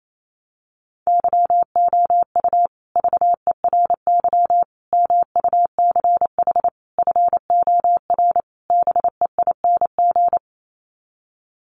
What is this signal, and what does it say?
Morse code, heard as a keyed tone, about 28 wpm, 700 Hz. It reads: YOU VERY MUCH FOR BEING
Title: Sets of 5 Words Encore 28wpm